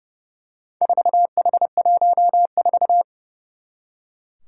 Morse code, heard as a keyed tone, about 32 wpm, 700 Hz. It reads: 4H14
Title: Four Challenging Letter-Number Combinations 30wpm